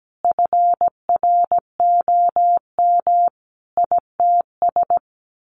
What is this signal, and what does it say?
Morse code, heard as a keyed tone, about 17 wpm, 700 Hz. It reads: FROM ITS
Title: A Sound Of Thunder 17wpm